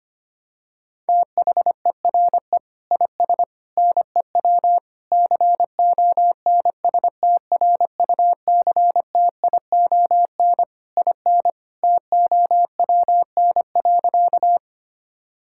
Morse code, heard as a keyed tone, about 25 wpm, 700 Hz. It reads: THERE IS NEW CONSTRUCTION IN TOWN.
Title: Sentences from Top 1600 Words 25wpm